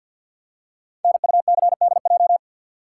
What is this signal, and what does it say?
Morse code, heard as a keyed tone, about 48 wpm, 700 Hz. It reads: NUCDJ
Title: Code Groups - 5 Letters - 50wpm